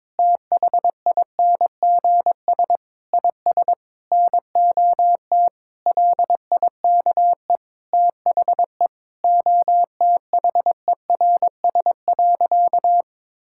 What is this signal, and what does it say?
Morse code, heard as a keyed tone, about 23 wpm, 700 Hz. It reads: THINGS IS NOT LIKE THE OTHERS.
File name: practice / 22wpm / Sentences with 8-Word Phrases - 22wpm